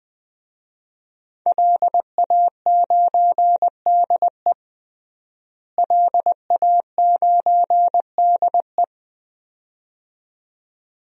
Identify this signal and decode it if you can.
Morse code, heard as a keyed tone, about 20 wpm, 700 Hz. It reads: LA9DE LA9DE
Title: DXCC-100 Call Signs - 5 Characters - Repeated 2x - 20wpm